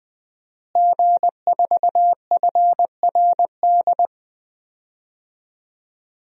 Morse code, heard as a keyed tone, about 20 wpm, 700 Hz. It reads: G4FRD
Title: DXCC-100 Call Signs - 5 Characters - Repeated 2x - 20wpm